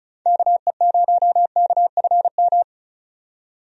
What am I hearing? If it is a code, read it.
Morse code, heard as a keyed tone, about 36 wpm, 700 Hz. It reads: KE0KFM
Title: USA Call Signs - 6 Characters - 35wpm